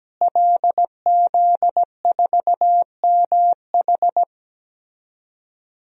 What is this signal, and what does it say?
Morse code, heard as a keyed tone, about 17 wpm, 700 Hz. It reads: LZ4MH
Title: DXCC-100 Call Signs - 5 Characters - Repeated 3x - 17wpm